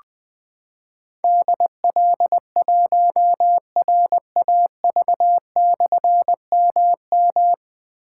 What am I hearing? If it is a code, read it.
Morse code, heard as a keyed tone, about 20 wpm, 700 Hz. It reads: DL1RAV/MM
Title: Difficult Call Signs 20wpm